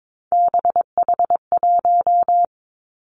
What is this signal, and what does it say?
Morse code, heard as a keyed tone, about 22 wpm, 700 Hz. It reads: BH1
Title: Three Challenging Letter-Number Combinations 22wpm